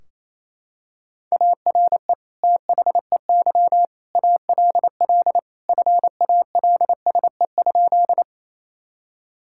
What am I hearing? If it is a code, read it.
Morse code, heard as a keyed tone, about 28 wpm, 700 Hz. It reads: ARE THEY ALL FALSE?
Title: Sentences from Top 1600 Words 28wpm